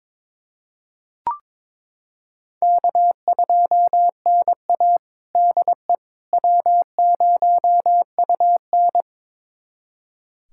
Morse code, heard as a keyed tone, about 22 wpm, 700 Hz. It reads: K2NA DE W0UN
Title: USA Callsign DE Callsign - 4 Characters - 22wpm